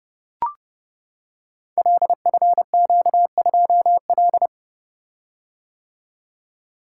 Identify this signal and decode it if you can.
Morse code, heard as a keyed tone, about 30 wpm, 700 Hz. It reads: LFQ2L
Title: Code Groups - 5 Letter:Numbers - 30wpm